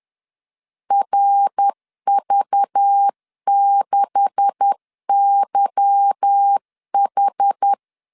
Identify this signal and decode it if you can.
Morse code, heard as a keyed tone, about 10 wpm, 800 Hz. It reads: RV6YH
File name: LEZIONI / 9 286 10wpm 2735